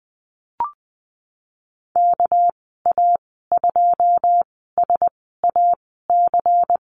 Morse code, heard as a keyed tone, about 20 wpm, spaced as for 15 wpm, 700 Hz. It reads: KA2SAC
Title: 200 Lesson 32g - Letter k - Call Signs 20wpm